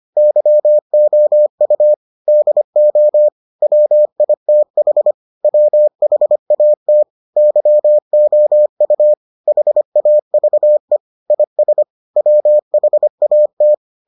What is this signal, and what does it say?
Morse code, heard as a keyed tone, about 25 wpm, 600 Hz. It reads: YOU DO WITH WHAT YOU HAVE IS WHAT